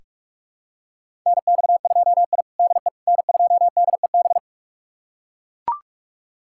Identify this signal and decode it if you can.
Morse code, heard as a keyed tone, about 42 wpm, 700 Hz. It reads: NX2I DE N2DEB
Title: USA Callsign DE Callsign 45wpm